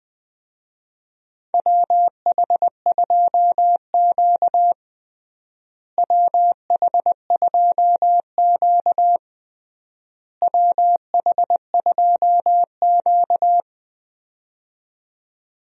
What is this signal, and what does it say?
Morse code, heard as a keyed tone, about 21 wpm, 700 Hz. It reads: WH2Q WH2Q WH2Q
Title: USA Call Signs - 4 Characters - Repeated 3x - 20wpm